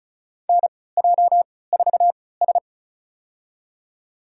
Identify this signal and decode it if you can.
Morse code, heard as a keyed tone, about 36 wpm, spaced as for 20 wpm, 700 Hz. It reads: NJ4S
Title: USA Call Signs - 3x Character Spacing - 35wpm